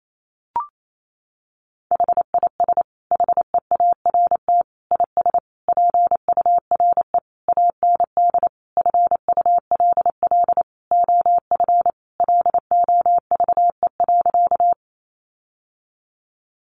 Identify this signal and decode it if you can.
Morse code, heard as a keyed tone, about 28 wpm, 700 Hz. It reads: HIS HEART IS PURE AND FULL OF LOVE.
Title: Sentences from Top 1500 Words 28wpm